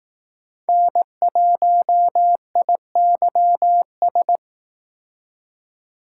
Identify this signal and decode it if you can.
Morse code, heard as a keyed tone, about 19 wpm, 700 Hz. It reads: N1IYS
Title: Speed Racing - USA Call Signs 15wpm